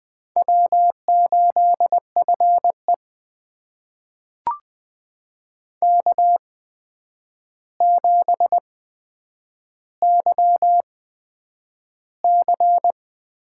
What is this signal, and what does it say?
Morse code, heard as a keyed tone, about 20 wpm, spaced as for 9 wpm, 700 Hz. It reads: W8FE K7YC
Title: USA Call Signs - 4 Characters - 8x Character Spacing - 20wpm